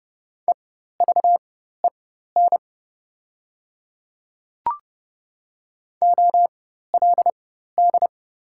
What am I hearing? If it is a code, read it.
Morse code, heard as a keyed tone, about 30 wpm, spaced as for 14 wpm, 700 Hz. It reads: EVEN OLD
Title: 4x Character Spacing - Top 300 Words - 30wpm